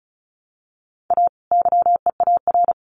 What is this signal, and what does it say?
Morse code, heard as a keyed tone, about 34 wpm, 700 Hz. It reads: A YEAR
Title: Sets of 2 Words 35wpm